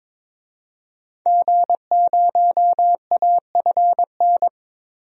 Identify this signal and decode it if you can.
Morse code, heard as a keyed tone, about 22 wpm, 700 Hz. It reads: G0AFN
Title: DXCC-100 Call Signs - 5 Characters - 22wpm